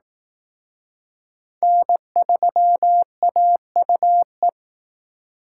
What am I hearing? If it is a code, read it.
Morse code, heard as a keyed tone, about 19 wpm, 700 Hz. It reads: N3AUE